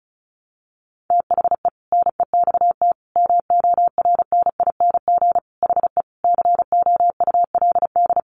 Morse code, heard as a keyed tone, about 34 wpm, 700 Hz. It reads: THE NEXT MORNING HE COULD
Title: Sentences from Top 600 Words 35wpm